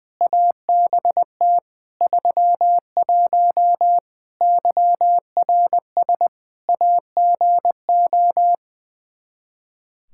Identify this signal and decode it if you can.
Morse code, heard as a keyed tone, about 21 wpm, 700 Hz. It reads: ABT 31 YRS AGO
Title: Practice QSO - Exchange 3 - More Info - 20wpm